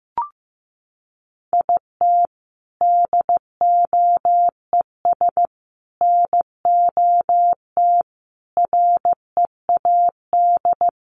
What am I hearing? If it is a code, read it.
Morse code, heard as a keyed tone, about 15 wpm, 700 Hz. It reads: IT DOES NOT READ
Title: Sentences from Top 800 Words 15wpm